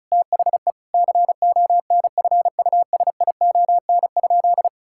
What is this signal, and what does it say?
Morse code, heard as a keyed tone, about 34 wpm, 700 Hz. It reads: THE CONFUSION?
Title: Sentences from Top 1600 Words 35wpm